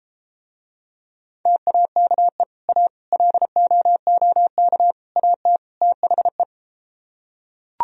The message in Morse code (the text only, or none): TAKE A LOOK AT THE